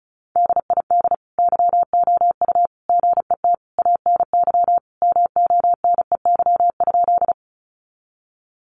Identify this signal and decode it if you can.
Morse code, heard as a keyed tone, about 34 wpm, 700 Hz. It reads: DID YOU GET ANY MONEY?
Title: Sentences from Top 400 Words 35wpm